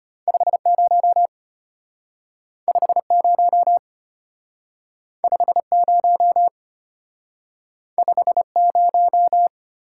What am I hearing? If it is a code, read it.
Morse code, heard as a keyed tone, about 32 wpm, 700 Hz. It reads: 50 50 50 50